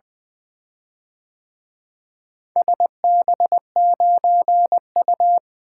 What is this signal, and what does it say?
Morse code, heard as a keyed tone, about 24 wpm, 700 Hz. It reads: SB9U
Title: Speed-Racing - Four Letter-Number Combinations - 20wpm